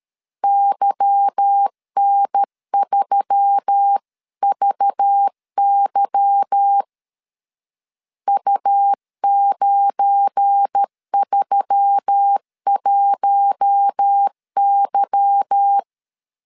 Morse code, heard as a keyed tone, about 13 wpm, 800 Hz. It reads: YN3VY U931Y
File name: LEZIONI / 8 YJ45 / 8 Y 12wpm 2228